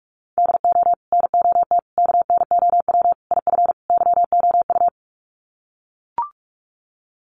Morse code, heard as a keyed tone, about 44 wpm, 700 Hz. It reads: DO NOT KNOW IF YOU
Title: Sets of 5 Words 45wpm